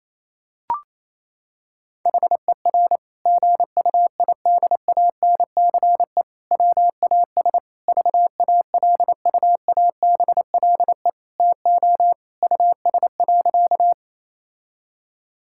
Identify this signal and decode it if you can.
Morse code, heard as a keyed tone, about 28 wpm, 700 Hz. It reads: HER GUIDANCE WAS VALUABLE TO US.
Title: Sentences from Top 1600 Words 28wpm